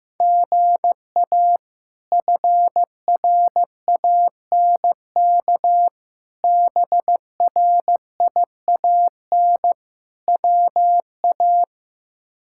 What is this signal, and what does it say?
Morse code, heard as a keyed tone, about 15 wpm, 700 Hz. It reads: GA FRANK BRIAN WA
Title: K1USN SST QSO Practice - 15wpm